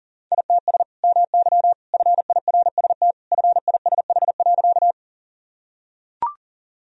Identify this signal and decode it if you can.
Morse code, heard as a keyed tone, about 40 wpm, 700 Hz. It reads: ITS MY FIRST FISH.
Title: Sentences from Top 1000 Words - Review Part 1 - 40wpm